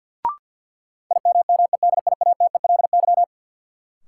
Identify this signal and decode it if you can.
Morse code, heard as a keyed tone, about 50 wpm, 700 Hz. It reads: IMMEDIATELY
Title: Sentences from Top 1100 Words - New Word Drill - 50wpm